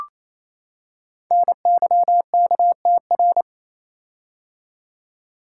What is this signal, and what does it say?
Morse code, heard as a keyed tone, about 27 wpm, 700 Hz. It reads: NYKTR
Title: Code Groups - 5 Letters - 28wpm